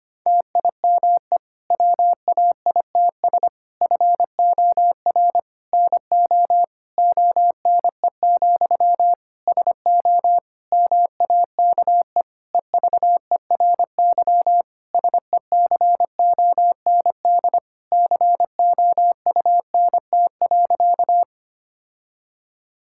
Morse code, heard as a keyed tone, about 25 wpm, 700 Hz. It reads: TIME WAITS FOR NO ONE, SO MAKE EVERY SECOND COUNT.
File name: practice / 25wpm / Difficult Sentences with Hints - 25wpm